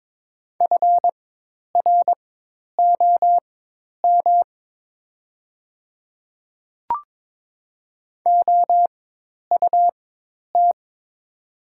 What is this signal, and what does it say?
Morse code, heard as a keyed tone, about 22 wpm, spaced as for 10 wpm, 700 Hz. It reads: FROM OUT
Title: 4x Character Spacing - Top 200 Words - 22wpm